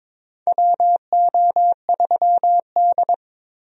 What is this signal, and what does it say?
Morse code, heard as a keyed tone, about 22 wpm, 700 Hz. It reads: WO3D
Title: USA Call Signs - 4 Characters - 22wpm